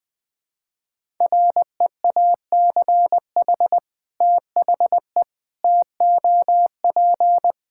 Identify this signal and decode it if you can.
Morse code, heard as a keyed tone, about 20 wpm, 700 Hz. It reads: REACH THE TOP
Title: Sentences with 2-Word Phrases - 20wpm